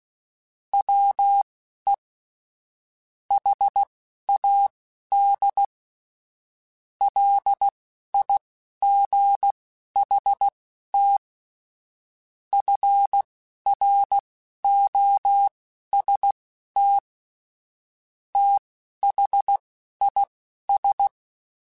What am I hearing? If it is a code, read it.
Morse code, heard as a keyed tone, about 16 wpm, spaced as for 11 wpm, 800 Hz. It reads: WE HAD LIGHT FROST THIS
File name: Weather Sets / Weather Sets-mp3s / Weather 4 28 @10